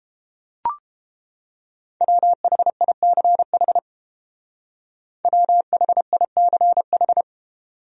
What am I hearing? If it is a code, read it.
Morse code, heard as a keyed tone, about 29 wpm, 700 Hz. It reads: WHICH WHICH